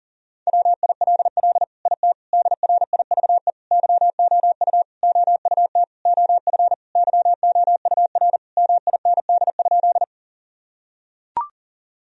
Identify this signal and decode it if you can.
Morse code, heard as a keyed tone, about 40 wpm, 700 Hz. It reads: WILL IT DRIVE YOU OUT OF YOUR MIND?
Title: Sentences from Top 1000 Words - Review Part 2 - 40wpm